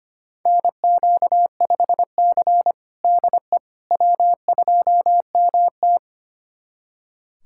Morse code, heard as a keyed tone, about 25 wpm, 700 Hz. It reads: NQ5C DE W2MT
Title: USA Callsign DE Callsign - 4 Characters - 25wpm